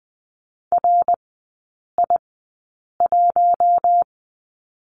Morse code, heard as a keyed tone, about 20 wpm, spaced as for 8 wpm, 700 Hz. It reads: RI1
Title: 053 Lesson 10e - Letter r - Three Characters 20wpm